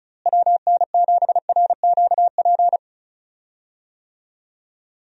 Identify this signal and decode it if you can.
Morse code, heard as a keyed tone, about 36 wpm, 700 Hz. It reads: WN7RQP
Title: USA Call Signs - 6 Characters - Repeated 2x - 35wpm